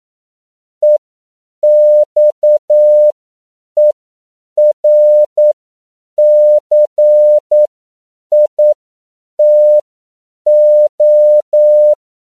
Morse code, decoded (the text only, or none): EXERCITO